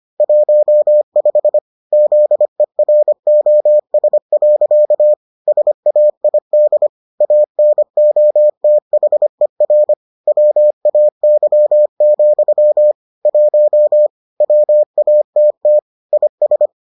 Morse code, heard as a keyed tone, about 25 wpm, 600 Hz. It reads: 15 ZEROS. SAID ANOTHER WAY, 1 WATT IS